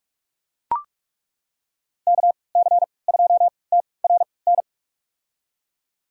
Morse code, heard as a keyed tone, about 41 wpm, spaced as for 25 wpm, 700 Hz. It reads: KC2TRN
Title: USA Call Signs - 6 Characters - 3x Character Spacing - 45wpm